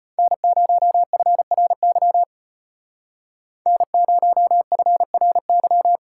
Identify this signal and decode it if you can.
Morse code, heard as a keyed tone, about 32 wpm, 700 Hz. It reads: N0FRY N0FRY